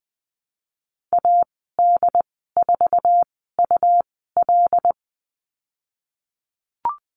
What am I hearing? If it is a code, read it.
Morse code, heard as a keyed tone, about 20 wpm, spaced as for 14 wpm, 700 Hz. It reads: AD4UL
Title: 154 Lesson 25g - Letter p - Call Signs 20wpm